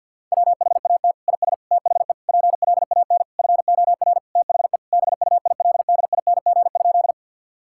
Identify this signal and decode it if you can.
Morse code, heard as a keyed tone, about 50 wpm, 700 Hz. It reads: WHAT IS THE PLAN FOR THE BUILDING?
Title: Sentences from Top 1000 Words - Review Part 5 - 50wpm